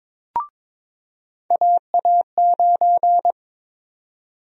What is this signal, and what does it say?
Morse code, heard as a keyed tone, about 22 wpm, 700 Hz. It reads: AA9
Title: USA Call Sign Prefixes - 22wpm